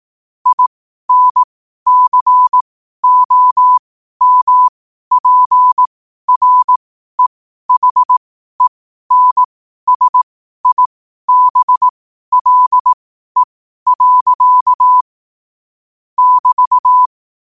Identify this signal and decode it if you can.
Morse code, heard as a keyed tone, about 18 wpm, spaced as for 12 wpm, 1000 Hz. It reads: INCOMPREHENSIBLE. <BT>
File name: 12wpm